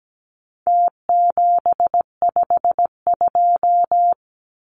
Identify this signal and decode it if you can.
Morse code, heard as a keyed tone, about 17 wpm, 700 Hz. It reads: T752
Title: Numbers - Contest Style - 4-Digit 17wpm